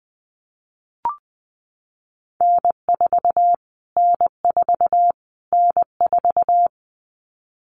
Morse code, heard as a keyed tone, about 20 wpm, 700 Hz. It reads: N4 N4 N4
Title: Callsign Prefixes for Top 25 DXCC - Repeated 3x - 20wpm